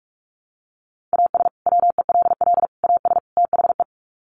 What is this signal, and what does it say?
Morse code, heard as a keyed tone, about 44 wpm, 700 Hz. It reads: AS WELL AS THE
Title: Sets of 4 Words 45wpm